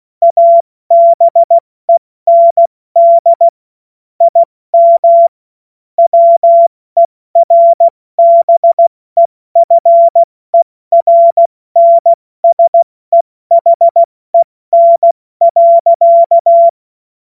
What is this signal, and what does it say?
Morse code, heard as a keyed tone, about 16 wpm, 680 Hz. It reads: ABEND IM WERBEFERNSEHEN.
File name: Morsekurs / Witze / Witz 27. April 2021, 70 F80